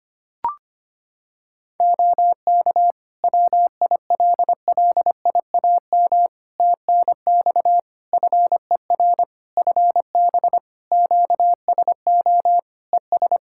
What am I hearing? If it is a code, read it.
Morse code, heard as a keyed tone, about 25 wpm, 700 Hz. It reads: OK WILLIAM TNX FER FB QSO ES
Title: Practice QSO - Exchange 4 - Ending - 25wpm